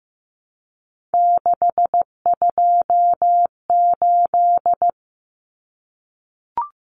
Morse code, heard as a keyed tone, about 15 wpm, 700 Hz. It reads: 628
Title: Common Cut Numbers - 3-Digit 15wpm